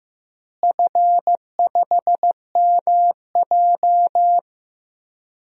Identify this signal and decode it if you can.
Morse code, heard as a keyed tone, about 15 wpm, 700 Hz. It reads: F5MJ
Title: DXCC-100 Call Signs - 4 Characters - Repeated 2x - 15wpm